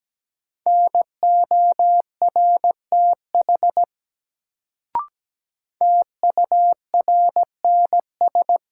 Morse code, heard as a keyed tone, about 17 wpm, 700 Hz. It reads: NORTH TURNS
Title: Sets of 5 Words Encore - New Word Drill - 17wpm